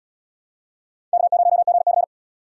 Morse code, heard as a keyed tone, about 77 wpm, 700 Hz. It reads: K0GC